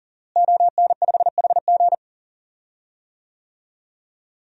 Morse code, heard as a keyed tone, about 40 wpm, 700 Hz. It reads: ON5HG
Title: DXCC-100 Call Signs - 5 Characters - Repeated 3x - 40wpm